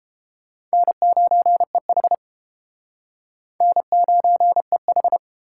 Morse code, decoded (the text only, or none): N9EH N9EH